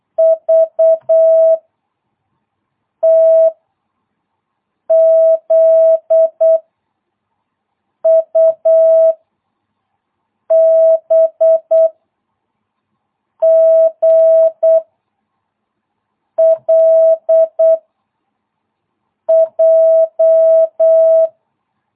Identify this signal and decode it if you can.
Morse code, heard as a keyed tone, about 8 wpm, 660 Hz. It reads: V T Z U B G L J